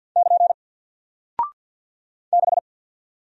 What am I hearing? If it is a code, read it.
Morse code, heard as a keyed tone, about 48 wpm, 700 Hz. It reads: <KN> 6
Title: Single Character (Letter:Number:Punctuation:Prosign) Alt 50wpm